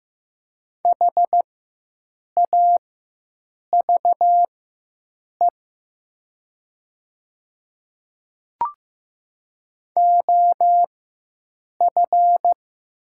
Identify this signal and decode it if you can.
Morse code, heard as a keyed tone, about 15 wpm, spaced as for 7 wpm, 700 Hz. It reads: HAVE OF
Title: 4x Character Spacing - Top 300 Words - 15wpm